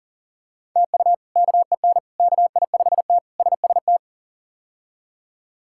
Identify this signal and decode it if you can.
Morse code, heard as a keyed tone, about 40 wpm, 700 Hz. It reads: TU KEN KI5T SST
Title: K1USN SST QSO Practice - 40wpm